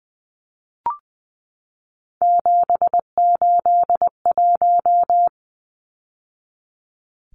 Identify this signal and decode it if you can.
Morse code, heard as a keyed tone, about 20 wpm, 700 Hz. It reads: 781
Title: Numbers - 3-Digit 20wpm